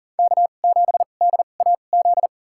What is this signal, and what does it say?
Morse code, heard as a keyed tone, about 40 wpm, spaced as for 29 wpm, 700 Hz. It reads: K7DAZ
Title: USA Call Signs - 5 Characters - 2x Character Spacing - 40wpm